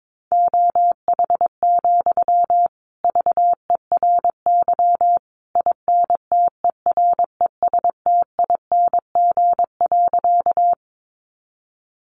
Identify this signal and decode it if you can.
Morse code, heard as a keyed tone, about 22 wpm, 700 Hz. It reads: OH, VERY INTERESTING.
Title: Sentences from Top 1000 Words 22wpm